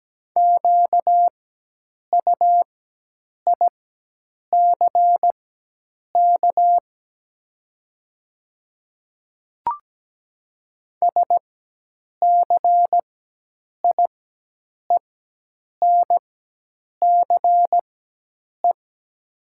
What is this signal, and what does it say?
Morse code, heard as a keyed tone, about 17 wpm, spaced as for 8 wpm, 700 Hz. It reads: QUICK SCIENCE
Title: 4x Character Spacing - Top 500 Words - 17wpm